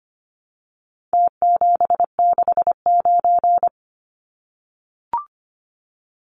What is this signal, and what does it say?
Morse code, heard as a keyed tone, about 25 wpm, 700 Hz. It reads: T769
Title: Numbers - Contest Style - 4-Digit 25wpm